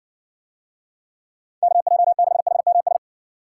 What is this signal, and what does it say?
Morse code, heard as a keyed tone, about 60 wpm, 700 Hz. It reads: KJ6HMS